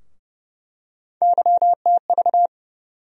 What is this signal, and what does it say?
Morse code, heard as a keyed tone, about 30 wpm, 700 Hz. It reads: YTV
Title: USA Call Sign Postfixes - 3 Letters 30wpm